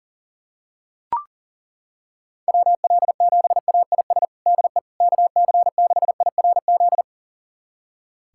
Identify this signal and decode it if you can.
Morse code, heard as a keyed tone, about 37 wpm, 700 Hz. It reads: WL7AIS DE KC6IRZ
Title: USA Callsign DE Callsign 40wpm